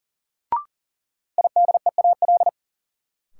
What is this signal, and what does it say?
Morse code, heard as a keyed tone, about 40 wpm, 700 Hz. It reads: IDEAL
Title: Sentences from Top 1400 Words - New Word Drill - 40wpm